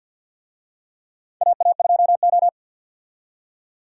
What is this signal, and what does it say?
Morse code, heard as a keyed tone, about 52 wpm, 700 Hz. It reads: AA2O
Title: USA Call Signs - 4 Characters - 50wpm